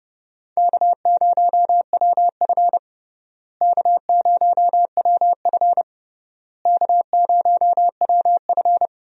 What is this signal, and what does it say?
Morse code, heard as a keyed tone, about 32 wpm, 700 Hz. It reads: K0WF K0WF K0WF
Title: USA Call Signs - 4 Characters - Repeated 3x - 30wpm